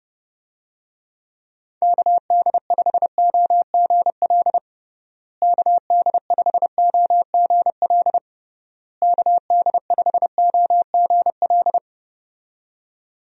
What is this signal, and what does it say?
Morse code, heard as a keyed tone, about 32 wpm, 700 Hz. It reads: KD5OGL KD5OGL KD5OGL
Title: USA Call Signs - 6 Characters - Repeated 3x - 30wpm